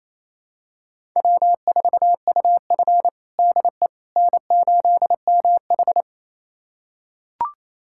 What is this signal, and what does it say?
Morse code, heard as a keyed tone, about 28 wpm, 700 Hz. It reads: W4UF DE N8MH
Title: USA Callsign DE Callsign - 4 Characters - 28wpm